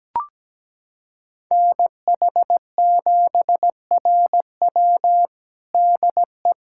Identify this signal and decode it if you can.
Morse code, heard as a keyed tone, about 17 wpm, 700 Hz. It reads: NH7RW DE
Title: USA Callsign DE Callsign 17wpm